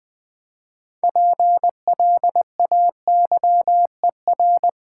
Morse code, heard as a keyed tone, about 20 wpm, 700 Hz. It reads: PLAYER
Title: Build A Word - 3 Letters At A Time - 6-Letter Words - 20wpm